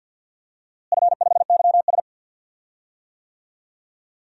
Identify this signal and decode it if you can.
Morse code, heard as a keyed tone, about 50 wpm, 700 Hz. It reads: F5YS
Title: DXCC-100 Call Signs - 4 Characters - Repeated 2x - 50wpm